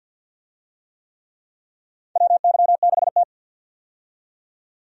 Morse code, heard as a keyed tone, about 47 wpm, 700 Hz. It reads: WY6T
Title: USA Call Signs - 4 Characters - 50wpm